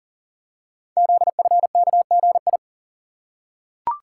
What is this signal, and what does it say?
Morse code, heard as a keyed tone, about 40 wpm, 690 Hz. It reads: ZFKGI
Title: Code Groups - 5 Letters - 40wpm